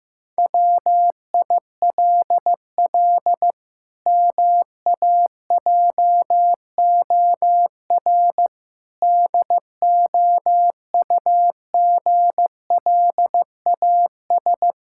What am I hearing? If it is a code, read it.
Morse code, heard as a keyed tone, about 15 wpm, 700 Hz. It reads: WILL MAJOR DOUGLAS
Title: Pangram Challenge v2 - 15wpm